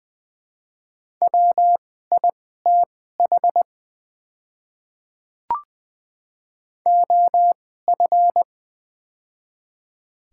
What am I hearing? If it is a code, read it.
Morse code, heard as a keyed tone, about 20 wpm, spaced as for 14 wpm, 700 Hz. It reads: WITH OF
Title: 2x Character Spacing - Top 100 Words - 20wpm